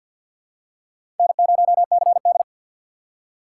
Morse code, heard as a keyed tone, about 52 wpm, 690 Hz. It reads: N0CD
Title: USA Call Signs - 6x Character Spacing - 50wpm